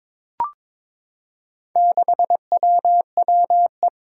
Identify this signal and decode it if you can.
Morse code, heard as a keyed tone, about 22 wpm, 700 Hz. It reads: 6WWE
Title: Four Challenging Letter-Number Combinations 22wpm